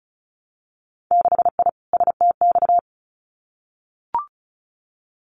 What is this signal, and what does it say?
Morse code, heard as a keyed tone, about 36 wpm, 700 Hz. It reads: 6I STX
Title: Winter Field Day Exchanges Encore 35wpm